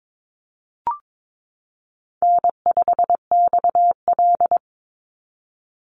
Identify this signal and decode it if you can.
Morse code, heard as a keyed tone, about 22 wpm, 700 Hz. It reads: N5XL
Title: NA SOTA Chaser Callsigns 22wpm